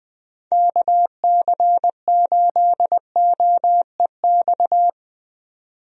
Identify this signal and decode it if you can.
Morse code, heard as a keyed tone, about 20 wpm, 700 Hz. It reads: KC8OEX
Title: USA Call Signs - 6 Characters - 20wpm